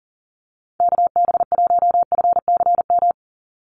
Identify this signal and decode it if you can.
Morse code, heard as a keyed tone, about 40 wpm, 700 Hz. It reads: KB1FCM
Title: USA Call Signs 40wpm